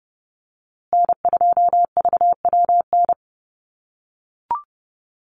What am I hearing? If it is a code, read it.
Morse code, heard as a keyed tone, about 30 wpm, 700 Hz. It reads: N2VWN
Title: USA Call Signs Repeated 3x 30wpm